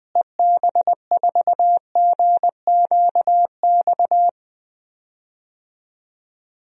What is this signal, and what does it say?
Morse code, heard as a keyed tone, about 20 wpm, 700 Hz. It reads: EB4GQX
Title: DXCC-100 Call Signs - 6 Characters - Repeated 2x - 20wpm